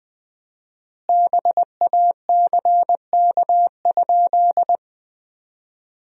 Morse code, heard as a keyed tone, about 20 wpm, 700 Hz. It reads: BACK?
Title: 4x Word-Spacing - Sentences from Top 200 Words - 20wpm